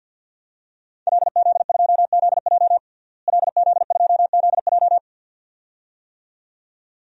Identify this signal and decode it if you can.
Morse code, heard as a keyed tone, about 52 wpm, 700 Hz. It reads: LZ2ZJ LZ2ZJ
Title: DXCC-100 Call Signs - 5 Characters - Repeated 2x - 50wpm